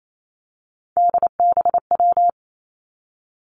Morse code, heard as a keyed tone, about 28 wpm, 700 Hz. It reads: DBW
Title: Three Challenging Letter-Number Combinations 28wpm